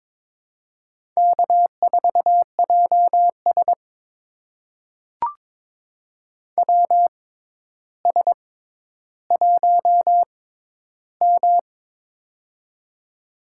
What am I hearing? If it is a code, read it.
Morse code, heard as a keyed tone, about 22 wpm, spaced as for 11 wpm, 700 Hz. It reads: K4JS WS1M
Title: USA Call Signs - 4 Characters - 6x Character Spacing - 22wpm